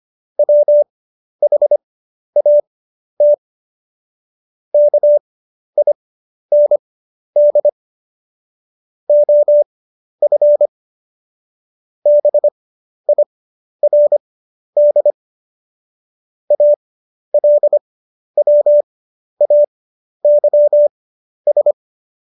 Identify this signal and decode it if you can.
Morse code, heard as a keyed tone, about 25 wpm, spaced as for 11 wpm, 590 Hz. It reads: WHAT KIND OF BIRD ALWAYS
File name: Jokes-25-char-10-eff